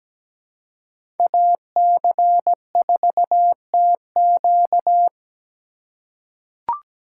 Morse code, heard as a keyed tone, about 17 wpm, 700 Hz. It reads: AC4TQ
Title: USA Call Signs - 5 Characters - 4x Character Spacing - 17wpm